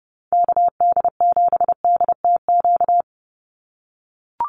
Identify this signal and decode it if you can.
Morse code, heard as a keyed tone, about 30 wpm, 700 Hz. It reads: KD7DTQ
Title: USA Call Signs 30wpm